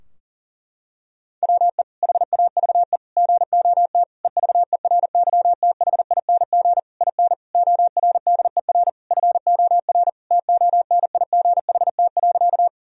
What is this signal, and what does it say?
Morse code, heard as a keyed tone, about 40 wpm, 700 Hz. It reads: WE HAVE GOT EVERYTHING IN ORDER FOR TONIGHT.
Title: Sentences from Top 1000 Words - Review Part 3 - 40wpm